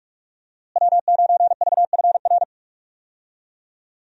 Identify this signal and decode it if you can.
Morse code, heard as a keyed tone, about 46 wpm, 700 Hz. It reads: W9VFR
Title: USA Call Signs - 5 Characters - Repeated 3x - 45wpm